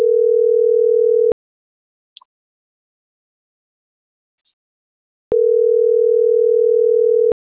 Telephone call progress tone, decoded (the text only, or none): ringback tone